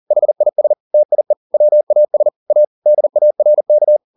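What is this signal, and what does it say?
Morse code, heard as a keyed tone, about 40 wpm, 600 Hz. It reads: HIS TIE WAS A DARK
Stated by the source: Morse code at 40 wpm, practice text, CW med Gnister.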